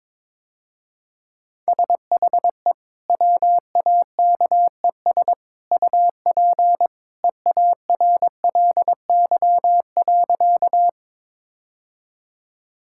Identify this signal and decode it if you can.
Morse code, heard as a keyed tone, about 22 wpm, 700 Hz. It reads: SHE WAKES UP EARLY.
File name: practice / 22wpm / Easy Sentences with Hints - 22wpm